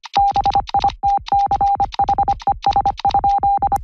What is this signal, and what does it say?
Morse code, heard as a keyed tone, about 25 wpm, 800 Hz. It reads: BITCHES?